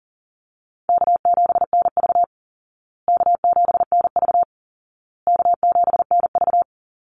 Morse code, heard as a keyed tone, about 40 wpm, 700 Hz. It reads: K7NV K7NV K7NV
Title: Easy Call Signs Repeated 3x - 40wpm